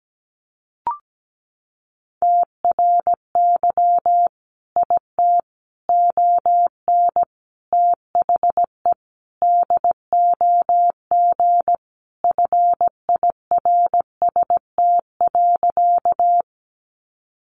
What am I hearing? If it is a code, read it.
Morse code, heard as a keyed tone, about 17 wpm, 700 Hz. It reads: TRY IT ON THE DOG FIRST.
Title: Sentences from Top 400 Words 17wpm